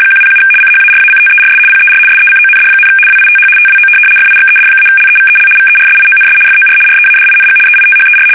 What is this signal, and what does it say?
RTTY, about 75 baud. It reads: 107 CHARACTERS PER SECOND, WITH TWO FIXED-LEVEL VERSIONS FOR READING EITHER
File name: fsk850-75B / teletype-corp